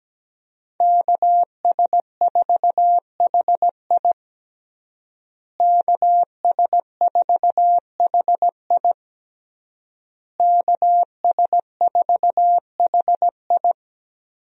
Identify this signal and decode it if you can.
Morse code, heard as a keyed tone, about 17 wpm, 700 Hz. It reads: KS4HI KS4HI KS4HI
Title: USA Call Signs - 5 Characters - Repeated 3x - 17wpm